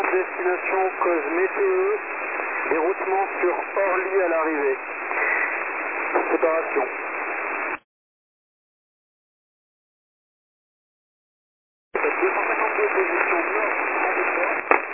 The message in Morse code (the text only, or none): none